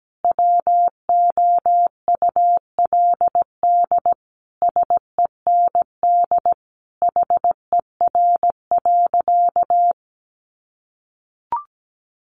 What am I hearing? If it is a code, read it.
Morse code, heard as a keyed tone, about 17 wpm, 700 Hz. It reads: WOULD SEND HER.
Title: Sentences from Top 1500 Words 17wpm